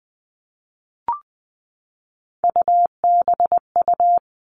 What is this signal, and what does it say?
Morse code, heard as a keyed tone, about 20 wpm, 700 Hz. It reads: UBU
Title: All Cut Numbers - 3-Digit 20wpm